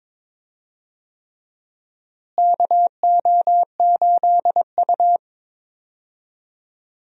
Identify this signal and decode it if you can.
Morse code, heard as a keyed tone, about 22 wpm, 700 Hz. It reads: KO8U
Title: USA Call Signs - 4 Characters - Repeated 3x - 22wpm